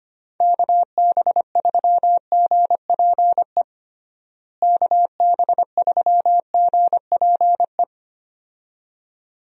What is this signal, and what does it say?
Morse code, heard as a keyed tone, about 25 wpm, 700 Hz. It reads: KB3GPE KB3GPE
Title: USA Call Signs - 6 Characters - Repeated 2x - 25wpm